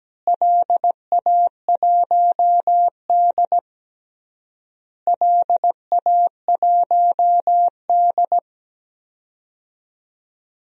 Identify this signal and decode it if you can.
Morse code, heard as a keyed tone, about 17 wpm, 700 Hz. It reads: LA1D LA1D
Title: DXCC-100 Call Signs - 4 Characters - Repeated 3x - 17wpm